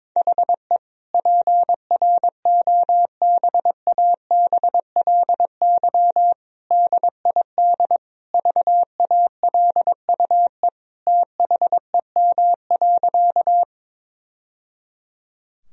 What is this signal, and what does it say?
Morse code, heard as a keyed tone, about 22 wpm, 700 Hz. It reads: HE PROBABLY DID VALUE THEM.
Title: Sentences from Top 1000 Words - Review Part 4 - 22wpm